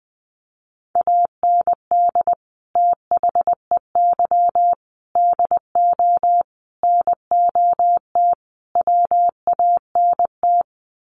Morse code, heard as a keyed tone, about 20 wpm, 700 Hz. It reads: AND THEY DO NOT WANT
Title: Sets of 5 Words Encore2 20wpm